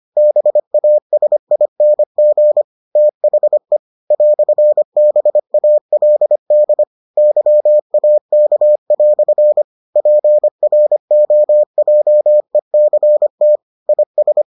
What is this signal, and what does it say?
Morse code, heard as a keyed tone, about 25 wpm, 600 Hz. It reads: BASING THE "BALD YAK" PROJECT IS